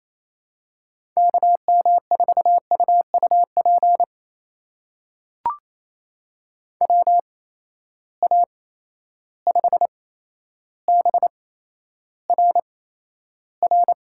Morse code, heard as a keyed tone, about 28 wpm, spaced as for 12 wpm, 700 Hz. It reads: KM4UUP WA5BRR
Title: USA Call Signs - 6 Characters - 8x Character Spacing - 28wpm